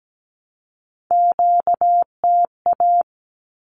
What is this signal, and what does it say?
Morse code, heard as a keyed tone, about 17 wpm, 700 Hz. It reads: QTA
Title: Q-Codes 17wpm